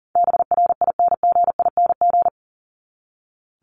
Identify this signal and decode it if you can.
Morse code, heard as a keyed tone, about 40 wpm, 700 Hz. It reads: BRINGING
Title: ING Suffix - Top 100 Words with Base Words 40wpm